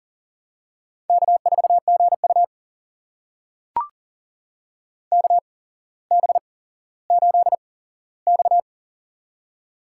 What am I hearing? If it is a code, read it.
Morse code, heard as a keyed tone, about 43 wpm, spaced as for 17 wpm, 700 Hz. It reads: K4GU KB8X
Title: USA Call Signs - 4 Characters - 8x Character Spacing - 40wpm